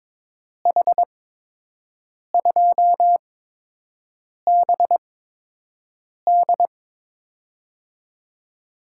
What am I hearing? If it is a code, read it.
Morse code, heard as a keyed tone, about 22 wpm, spaced as for 6 wpm, 700 Hz. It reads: H2BD
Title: USA Call Signs - 5 Characters - 8x Character Spacing - 22wpm